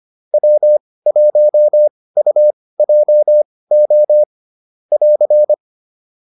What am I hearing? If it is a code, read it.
Morse code, heard as a keyed tone, about 25 wpm, spaced as for 18 wpm, 600 Hz. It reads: W1UJO <AR>
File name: QSO401 18